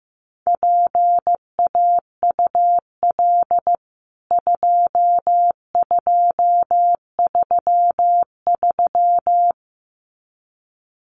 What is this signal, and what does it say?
Morse code, heard as a keyed tone, about 14 wpm, 700 Hz. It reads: PAUL 2233
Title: CWT Exchange 15wpm